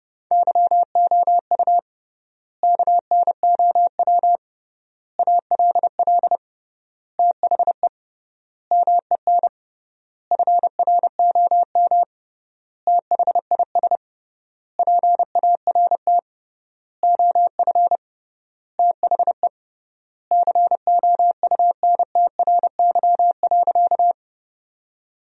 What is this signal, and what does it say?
Morse code, heard as a keyed tone, about 30 wpm, 700 Hz. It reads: YOU KNOW ALL THE MEN FROM THIS PART OF THE COUNTRY.
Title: 3x Word-Spacing - Sentences from Top 300 Words - 30wpm